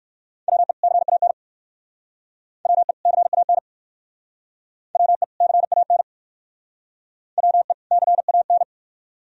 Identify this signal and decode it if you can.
Morse code, heard as a keyed tone, about 55 wpm, 700 Hz. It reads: WE CAN WE CAN WE CAN WE CAN
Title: Speed-Racing - Sets of 2 Words 45wpm